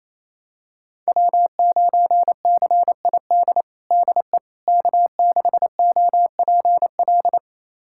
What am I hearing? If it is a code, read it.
Morse code, heard as a keyed tone, about 28 wpm, 700 Hz. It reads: W9CID DE K6OPL
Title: USA Callsign DE Callsign - 5 Characters - 28wpm